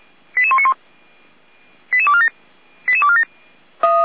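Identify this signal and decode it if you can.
Five-tone selective call (ZVEI), heard as quick runs of five tones, tones of about 70 ms. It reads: 88191 88247 88247
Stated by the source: ZVEI five-tone selective call 88102 received on SDR, on the Feuerwehr Orlamünde (f85360) audio server.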